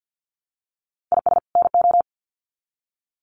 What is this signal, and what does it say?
Morse code, heard as a keyed tone, about 50 wpm, 700 Hz. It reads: IS NO
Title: Sets of 2 Words 50wpm